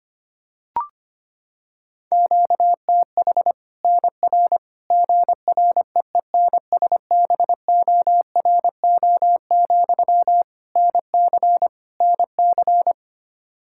Morse code, heard as a keyed tone, about 25 wpm, 700 Hz. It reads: QTH NR GREENSBORO, NC NC
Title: Practice QSO - Exchange 1 - Beginning - 25wpm